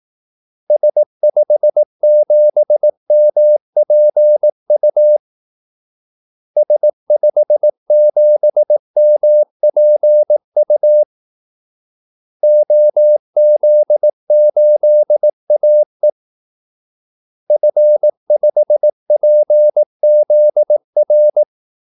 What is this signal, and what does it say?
Morse code, heard as a keyed tone, about 18 wpm, 600 Hz. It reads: S57MPU S57MPU OZ8AE F5PZR
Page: Kaldesignaler 18 wpm